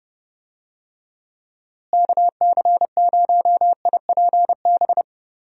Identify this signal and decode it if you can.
Morse code, heard as a keyed tone, about 30 wpm, 700 Hz. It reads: KC0IPB
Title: USA Call Signs - 6 Characters - Repeated 2x - 30wpm